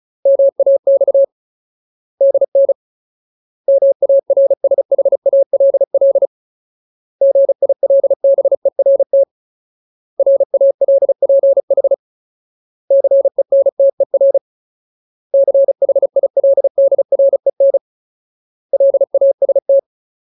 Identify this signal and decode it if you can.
Morse code, heard as a keyed tone, about 36 wpm, 550 Hz. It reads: MAX DN MARSHALL GILBERT RALPH CENTER CHILDREN LAST
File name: Ultimate Word List / 35WPM Ultimate